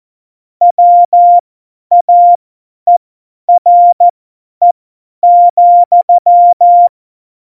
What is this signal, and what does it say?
Morse code, heard as a keyed tone, about 14 wpm, spaced as for 10 wpm, 700 Hz. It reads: WAERE,
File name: Morsekurs / Witze / Witz 26. Februar 2019, 50 F70